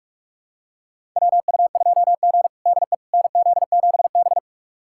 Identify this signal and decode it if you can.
Morse code, heard as a keyed tone, about 45 wpm, 700 Hz. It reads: WU2G DE NZ7B
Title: USA Callsign DE Callsign - 4 Characters - 45wpm